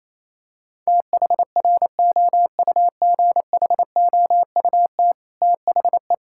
Morse code, heard as a keyed tone, about 28 wpm, 700 Hz. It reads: THROUGHOUT THE
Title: Sentences from Top 1000 Words - Review Part 5 - 28wpm